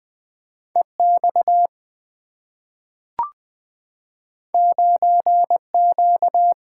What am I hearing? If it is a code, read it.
Morse code, heard as a keyed tone, about 20 wpm, 700 Hz. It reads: EX 9Q
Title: Two Letter-Number Combinations - 20wpm